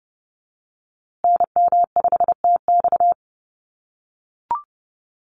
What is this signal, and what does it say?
Morse code, heard as a keyed tone, about 30 wpm, 700 Hz. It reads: NM5TX
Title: USA Call Signs Repeated 2x 30wpm